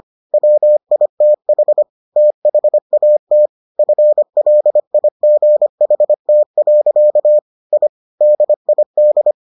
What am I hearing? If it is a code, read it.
Morse code, heard as a keyed tone, about 25 wpm, 600 Hz. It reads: WITH THAT FLIGHT. I DID